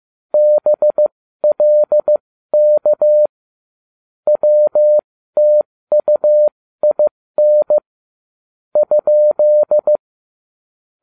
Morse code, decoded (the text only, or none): BLK WTUIN ?